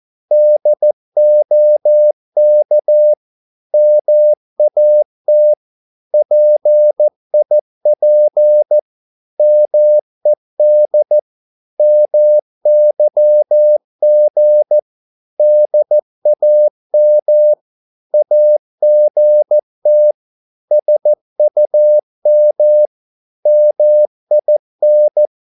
Morse code, heard as a keyed tone, about 14 wpm, 600 Hz. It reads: DOK MAT PIP MED MYG DAM AGT SUM MIN
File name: Korte ord DK 14wpm